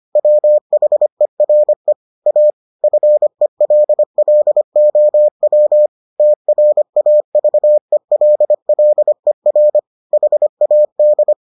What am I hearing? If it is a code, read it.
Morse code, heard as a keyed tone, about 25 wpm, 600 Hz. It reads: WHERE A FELLOW TRAVELLER HAD